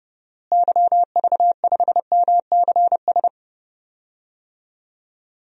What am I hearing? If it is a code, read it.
Morse code, heard as a keyed tone, about 31 wpm, 700 Hz. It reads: YV5MCS
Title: DXCC-100 Call Signs - 6 Characters - Repeated 2x - 30wpm